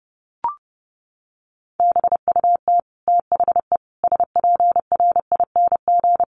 Morse code, heard as a keyed tone, about 30 wpm, 700 Hz. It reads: BUT THE SPRING
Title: Sentences from Top 1500 Words 30wpm